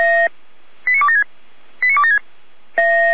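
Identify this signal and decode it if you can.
Five-tone selective call (ZVEI), heard as quick runs of five tones, tones of about 70 ms. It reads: 89276 89276
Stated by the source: ZVEI five-tone selective call received by SDR.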